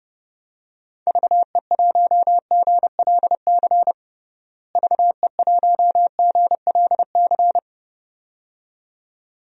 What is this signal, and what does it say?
Morse code, heard as a keyed tone, about 29 wpm, 700 Hz. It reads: VE1GLC VE1GLC
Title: DXCC-100 Call Signs - 6 Characters - Repeated 2x - 30wpm